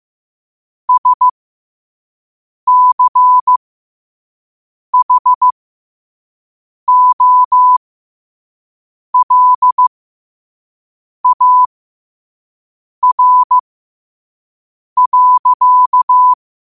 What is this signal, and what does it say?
Morse code, heard as a keyed tone, about 15 wpm, spaced as for 5 wpm, 1000 Hz. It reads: SCHOLAR.
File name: qotdfiles / 5wpm